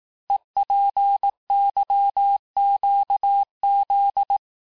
Morse code, heard as a keyed tone, about 18 wpm, 800 Hz. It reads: EPYQZ
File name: cours-du-lundi files / cours-1594--teste-chiffre-vitesse-17mpm